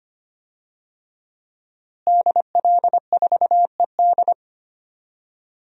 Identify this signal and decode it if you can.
Morse code, heard as a keyed tone, about 25 wpm, 700 Hz. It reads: DL4ED
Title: DXCC-100 Call Signs - 5 Characters - Repeated 2x - 25wpm